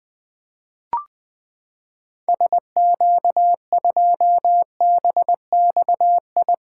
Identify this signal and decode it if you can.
Morse code, heard as a keyed tone, about 20 wpm, 700 Hz. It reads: SQ2BXI
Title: 2023 POTA Activator Call Signs - 20wpm